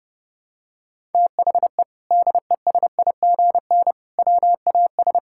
Morse code, heard as a keyed tone, about 30 wpm, 700 Hz. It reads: THE DESIGN WAS
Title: Sentences from Top 1000 Words - Review Part 5 - 30wpm